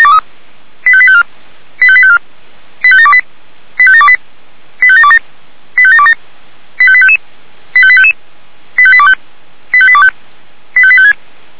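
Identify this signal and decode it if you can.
Five-tone selective call (ZVEI), heard as quick runs of five tones, tones of about 70 ms. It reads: 85854 85854 85728 85728 85727 85727 85855 85855 85825 85825 85856
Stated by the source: Five-tone ZVEI selective call received on an SDR.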